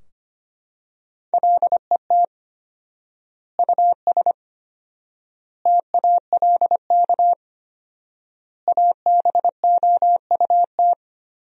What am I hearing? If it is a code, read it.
Morse code, heard as a keyed tone, about 25 wpm, 700 Hz. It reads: LET US TALK ABOUT
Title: 4x Word-Spacing - Sentences from Top 400 Words - 25wpm